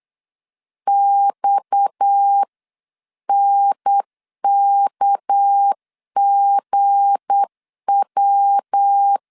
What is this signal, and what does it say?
Morse code, heard as a keyed tone, about 8 wpm, 800 Hz. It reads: XNKGW
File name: CW / DET / 1 9 1500C S C 8wpm 5653